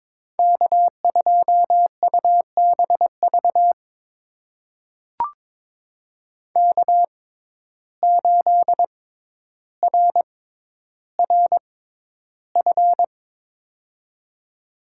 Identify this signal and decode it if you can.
Morse code, heard as a keyed tone, about 22 wpm, spaced as for 11 wpm, 700 Hz. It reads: K2UBV K8RRF
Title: USA Call Signs - 5 Characters - 6x Character Spacing - 22wpm